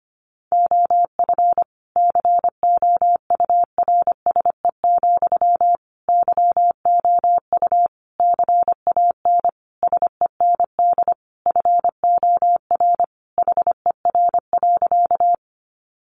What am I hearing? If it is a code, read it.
Morse code, heard as a keyed tone, about 25 wpm, 700 Hz. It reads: OF COURSE, YOU CAN SEND FOR HER.
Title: Sentences from Top 1500 Words 25wpm